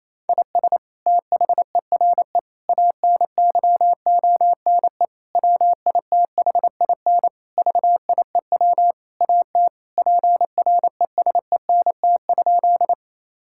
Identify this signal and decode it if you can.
Morse code, heard as a keyed tone, about 28 wpm, 700 Hz. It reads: IS THERE ANYONE WITHIN VIEW AT PRESENT?
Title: Sentences from Top 1000 Words - Review Part 6 - 28wpm